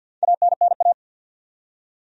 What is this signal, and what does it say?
Morse code, heard as a keyed tone, about 46 wpm, 700 Hz. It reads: ANNA
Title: USA Common Names - Max 4 Letters - 50wpm